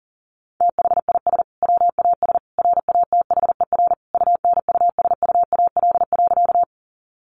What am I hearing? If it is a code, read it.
Morse code, heard as a keyed tone, about 40 wpm, 700 Hz. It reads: THIS WAS RATHER UNUSUAL.
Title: Sentences from Top 1400 Words 40wpm